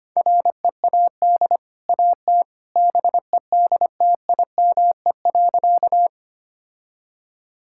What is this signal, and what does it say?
Morse code, heard as a keyed tone, about 25 wpm, 700 Hz. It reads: READ AT BEDTIME.
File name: Easy Sentences with Hints - 25wpm